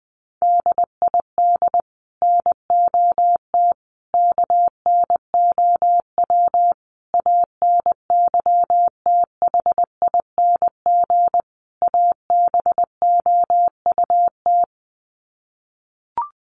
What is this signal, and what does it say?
Morse code, heard as a keyed tone, about 20 wpm, 700 Hz. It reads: DID NOT KNOW ANYTHING ABOUT
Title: Sets of 5 Words Encore2 20wpm